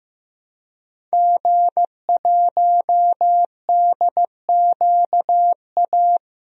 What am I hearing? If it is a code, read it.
Morse code, heard as a keyed tone, about 15 wpm, 700 Hz. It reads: G1DQA
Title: DXCC-100 Call Signs - 5 Characters - Repeated 2x - 15wpm